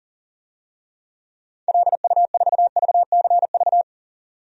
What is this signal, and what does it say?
Morse code, heard as a keyed tone, about 40 wpm, 700 Hz. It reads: LU4VCV
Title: DXCC-100 Call Signs - 6 Characters - 40wpm